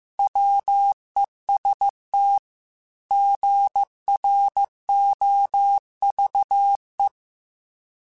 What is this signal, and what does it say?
Morse code, heard as a keyed tone, about 15 wpm, 800 Hz. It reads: WEST GROVE